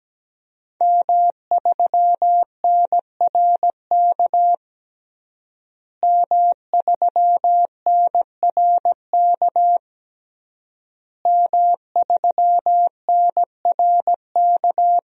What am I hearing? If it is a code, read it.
Morse code, heard as a keyed tone, about 17 wpm, 700 Hz. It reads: M3NRK M3NRK M3NRK
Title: DXCC-100 Call Signs - 5 Characters - Repeated 3x - 17wpm